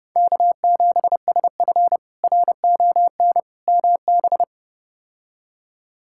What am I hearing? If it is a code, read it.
Morse code, heard as a keyed tone, about 30 wpm, 700 Hz. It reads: K7SF RON MB
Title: K1USN SST QSO Practice - 30wpm